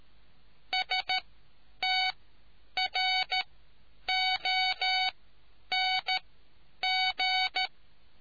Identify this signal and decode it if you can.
Morse code, heard as a keyed tone, about 13 wpm, spaced as for 9 wpm, 750 Hz. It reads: STRONG